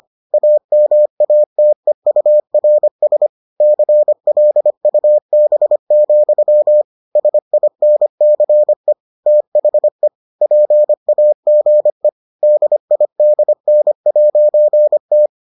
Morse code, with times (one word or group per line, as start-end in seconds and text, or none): 0.3-3.3 AMATEURS
3.6-6.8 CLUB,
7.2-8.9 SINCE
9.3-10.1 THE
10.4-12.1 PAGE
12.4-15.3 DIDN'T